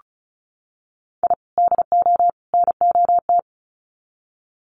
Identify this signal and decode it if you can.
Morse code, heard as a keyed tone, about 34 wpm, 700 Hz. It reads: I DO NOT
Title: Sets of 3 Words 35wpm